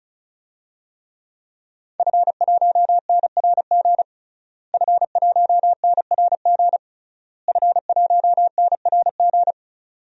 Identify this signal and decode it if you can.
Morse code, heard as a keyed tone, about 34 wpm, 700 Hz. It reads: F1NRG F1NRG F1NRG
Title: DXCC-100 Call Signs - 5 Characters - Repeated 3x - 35wpm